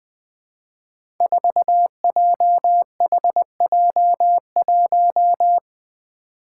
Morse code, heard as a keyed tone, about 20 wpm, 700 Hz. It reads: 4JHJ1
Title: Five Challenging Letter-Number Combinations 20wpm